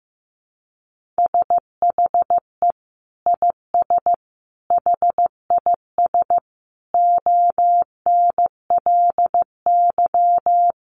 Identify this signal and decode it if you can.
Morse code, heard as a keyed tone, about 15 wpm, 700 Hz. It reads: SHE IS HIS ONLY
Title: Sentences from Top 1500 Words 15wpm